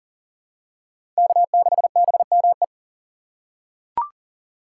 Morse code, heard as a keyed tone, about 37 wpm, 700 Hz. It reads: K6BME
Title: USA Call Signs - 5 Characters - 2x Character Spacing - 40wpm